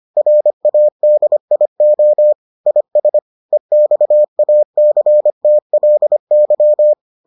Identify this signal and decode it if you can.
Morse code, heard as a keyed tone, about 25 wpm, 600 Hz. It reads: RADIO IS EXACTLY